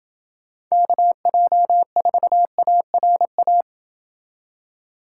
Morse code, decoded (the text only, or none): KJ4ARA